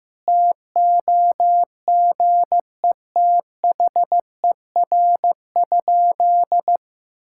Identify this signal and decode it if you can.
Morse code, heard as a keyed tone, about 15 wpm, 700 Hz. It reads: TOGETHER?
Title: Sentences from Top 1600 Words 15wpm